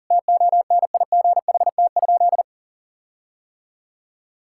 Morse code, heard as a keyed tone, about 40 wpm, 700 Hz. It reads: TONIGHT?
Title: Sentences from Top 1600 Words 40wpm